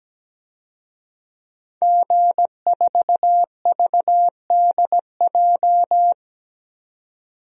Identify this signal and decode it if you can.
Morse code, heard as a keyed tone, about 17 wpm, 700 Hz. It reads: G4VDJ
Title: DXCC-100 Call Signs - 5 Characters - 17wpm